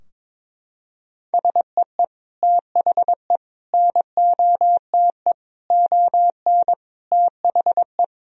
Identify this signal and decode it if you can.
Morse code, heard as a keyed tone, about 22 wpm, 700 Hz. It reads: SEE THE NOTE ON THE
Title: Sentences from Top 1000 Words - Review Part 2 - 22wpm